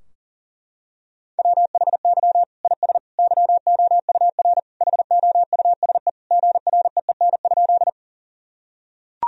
Morse code, heard as a keyed tone, about 40 wpm, 700 Hz. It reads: WHY IS YOUR HOUSE GREEN?
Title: Sentences from Top 1000 Words - Review Part 2 - 40wpm